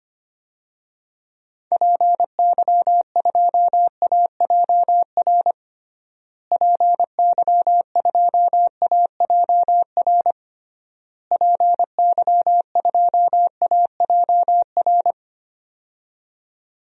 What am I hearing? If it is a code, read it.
Morse code, heard as a keyed tone, about 25 wpm, 700 Hz. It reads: PY2AJR PY2AJR PY2AJR
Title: DXCC-100 Call Signs - 6 Characters - Repeated 3x - 25wpm